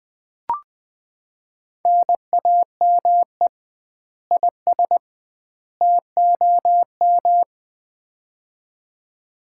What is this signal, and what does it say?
Morse code, heard as a keyed tone, about 20 wpm, 700 Hz. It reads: NAME IS TOM
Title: Common QSO Phrases - 2x Word-Spacing - 20wpm